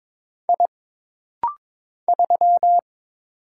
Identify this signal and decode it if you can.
Morse code, heard as a keyed tone, about 22 wpm, 700 Hz. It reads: I 3
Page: This rapid-fire timing is geared to push your ICR (Instant Character Recognition) skills.